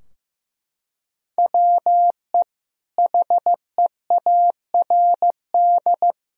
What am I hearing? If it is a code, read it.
Morse code, heard as a keyed tone, about 14 wpm, 700 Hz. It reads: WE HEARD
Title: Sentences from Top 1000 Words - Review Part 3 - 15wpm